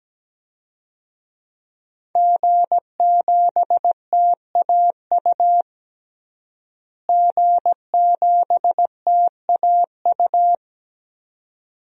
Morse code, heard as a keyed tone, about 17 wpm, 700 Hz. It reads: G7TAU G7TAU
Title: DXCC-100 Call Signs - 5 Characters - Repeated 2x - 17wpm